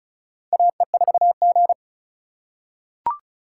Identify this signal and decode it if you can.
Morse code, heard as a keyed tone, about 36 wpm, 700 Hz. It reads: AE4G
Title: USA Call Signs - 4 Characters - 2x Character Spacing - 35wpm